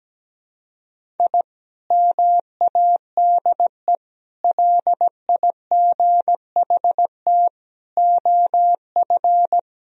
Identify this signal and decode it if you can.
Morse code, heard as a keyed tone, about 17 wpm, 700 Hz. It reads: I MADE LIGHT OF
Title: Sentences from Top 1000 Words - Review Part 1 - 17wpm